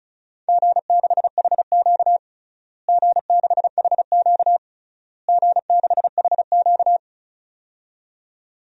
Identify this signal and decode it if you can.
Morse code, heard as a keyed tone, about 34 wpm, 700 Hz. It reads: G6HQ G6HQ G6HQ
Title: DXCC-100 Call Signs - 4 Characters - Repeated 3x - 35wpm